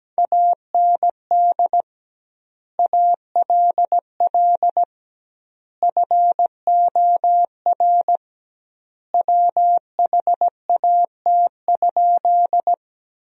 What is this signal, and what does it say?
Morse code, heard as a keyed tone, about 17 wpm, 700 Hz. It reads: AND ALL FOR WHAT?
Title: 2x Word-Spacing - Sentences from Top 100 Words - 17wpm